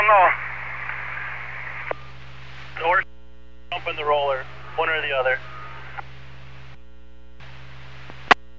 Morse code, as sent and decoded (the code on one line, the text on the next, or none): none